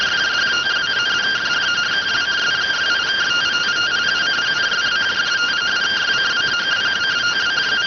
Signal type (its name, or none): two-tone teleprinter signal